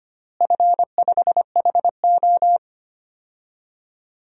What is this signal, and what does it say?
Morse code, heard as a keyed tone, about 25 wpm, 700 Hz. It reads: F5HO
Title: DXCC-100 Call Signs - 4 Characters - Repeated 3x - 25wpm